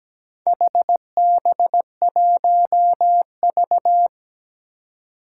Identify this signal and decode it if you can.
Morse code, heard as a keyed tone, about 17 wpm, 700 Hz. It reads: HB1V
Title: Four Challenging Letter-Number Combinations 17wpm